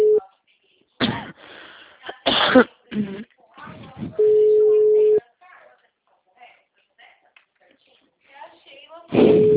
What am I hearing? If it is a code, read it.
Ringback tone (Europe, 425 Hz), with speech.